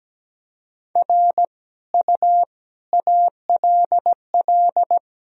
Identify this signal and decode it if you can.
Morse code, heard as a keyed tone, about 17 wpm, 700 Hz. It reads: R U ALL
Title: Sentences from Top 400 Words and Abbreviations 17wpm